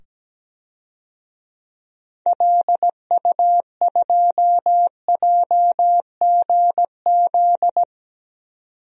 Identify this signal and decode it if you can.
Morse code, heard as a keyed tone, about 17 wpm, 700 Hz. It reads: LU2JGZ
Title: DXCC-100 Call Signs - 6 Characters - 17wpm